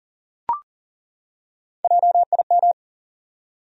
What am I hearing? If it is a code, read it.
Morse code, heard as a keyed tone, about 39 wpm, 700 Hz. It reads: JIM
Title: USA Common Names - Max 3 Letters - 40wpm